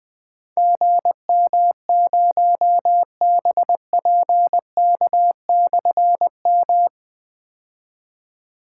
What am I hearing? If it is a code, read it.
Morse code, heard as a keyed tone, about 20 wpm, 700 Hz. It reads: GM0BPK/M
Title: Difficult Call Signs 20wpm